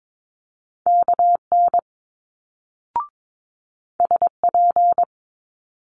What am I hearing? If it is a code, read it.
Morse code, heard as a keyed tone, about 22 wpm, 700 Hz. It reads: KN SP
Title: Multi-letter Phonemes Quick Fire 22wpm